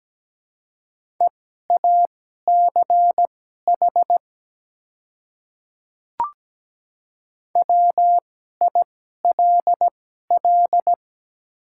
Morse code, heard as a keyed tone, about 17 wpm, spaced as for 12 wpm, 700 Hz. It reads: EACH WILL
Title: 2x Character Spacing - Top 300 Words - 17wpm